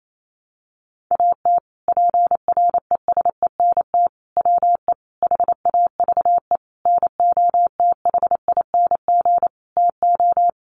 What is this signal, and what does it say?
Morse code, heard as a keyed tone, about 28 wpm, 700 Hz. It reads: AT PRESENT WE HAVE NOTHING TO
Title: Sentences from Top 800 Words 28wpm